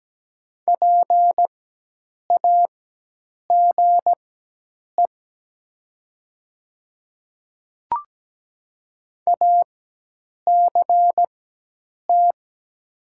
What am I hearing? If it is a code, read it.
Morse code, heard as a keyed tone, about 17 wpm, spaced as for 8 wpm, 700 Hz. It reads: PAGE ACT
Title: 4x Character Spacing - Top 500 Words - 17wpm